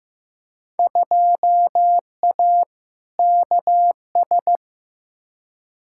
Morse code, heard as a keyed tone, about 15 wpm, 700 Hz. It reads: 2A KS
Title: Field Day Exchanges - 15wpm